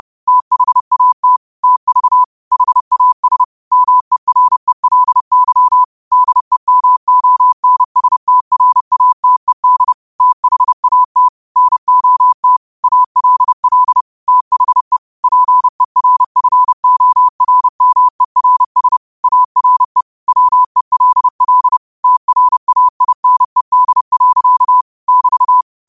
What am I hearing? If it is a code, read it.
Morse code, heard as a keyed tone, about 30 wpm, 1000 Hz. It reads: THAT TV HAS MERELY DEMONSTRATED THAT NOT ALL THE PERFORMERS ARE WELL TRAINED. <BT>
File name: qotdfiles / 30wpm